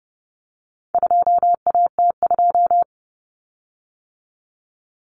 Morse code, heard as a keyed tone, about 30 wpm, 700 Hz. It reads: 2AT2
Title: Common Cut Numbers - 4-Digit 30wpm